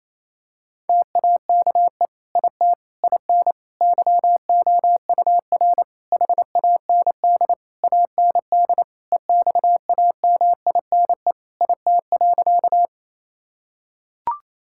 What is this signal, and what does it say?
Morse code, heard as a keyed tone, about 28 wpm, 700 Hz. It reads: TAKE IT IN YOUR HAND AND EXAMINE IT.
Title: Sentences from Top 1600 Words 28wpm